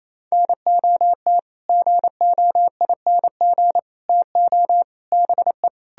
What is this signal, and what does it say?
Morse code, heard as a keyed tone, about 37 wpm, 700 Hz. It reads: NOT GOING TO BE